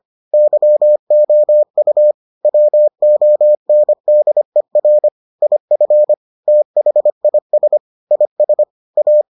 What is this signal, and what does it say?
Morse code, heard as a keyed tone, about 25 wpm, 600 Hz. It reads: YOU WONDER IF THIS IS A